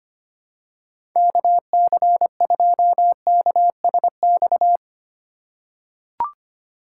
Morse code, heard as a keyed tone, about 25 wpm, 700 Hz. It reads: KC2KSX
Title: USA Call Signs - 6 Characters - 4x Character Spacing - 25wpm